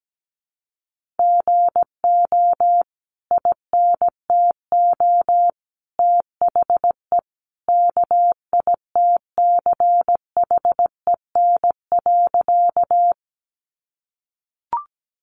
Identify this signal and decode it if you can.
Morse code, heard as a keyed tone, about 17 wpm, 700 Hz. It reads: GO INTO THE KITCHEN.
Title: Sentences from Top 1400 Words 17wpm